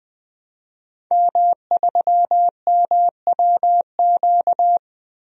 Morse code, heard as a keyed tone, about 20 wpm, 700 Hz. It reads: M3MWQ
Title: DXCC-100 Call Signs - 5 Characters - 20wpm